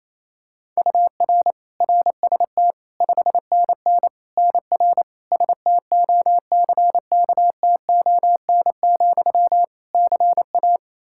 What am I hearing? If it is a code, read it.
Morse code, heard as a keyed tone, about 28 wpm, 700 Hz. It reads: UR RST 5NN NR STOCKTON, CA
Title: Practice QSO - Exchange 1 - Beginning - 28wpm